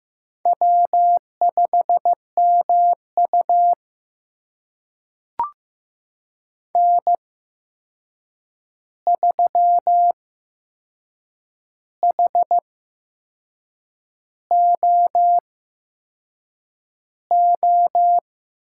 Morse code, heard as a keyed tone, about 15 wpm, spaced as for 6 wpm, 700 Hz. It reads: W5MU N3HOO
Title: USA Call Signs - 8x Character Spacing - 15wpm